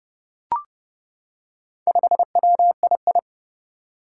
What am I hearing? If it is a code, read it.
Morse code, heard as a keyed tone, about 30 wpm, 700 Hz. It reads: 5WII
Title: Four Challenging Letter-Number Combinations 30wpm